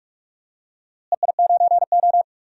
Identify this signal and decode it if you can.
Morse code, heard as a keyed tone, about 41 wpm, 700 Hz. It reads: EI9O
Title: DXCC-100 Call Signs - 4 Characters - Repeated 3x - 45wpm